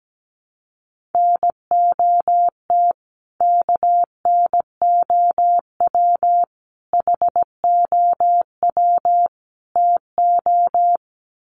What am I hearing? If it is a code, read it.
Morse code, heard as a keyed tone, about 17 wpm, 700 Hz. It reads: NOT KNOW HOW TO
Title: Sets of 4 Words 17wpm